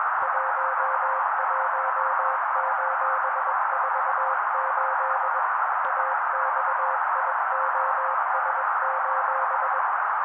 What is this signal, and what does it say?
Morse code, heard as a keyed tone, about 21 wpm, 530 Hz. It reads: 11:48 AXIOS:
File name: daily / 20wpm noise